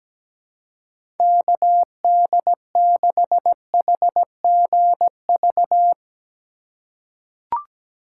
Morse code, heard as a keyed tone, about 17 wpm, 700 Hz. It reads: KD6HGV
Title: USA Call Signs - 6 Characters - 4x Character Spacing - 17wpm